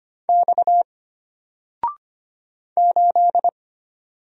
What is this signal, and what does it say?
Morse code, heard as a keyed tone, about 25 wpm, 700 Hz. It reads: X 8
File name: Single-Letter-Number-25wpm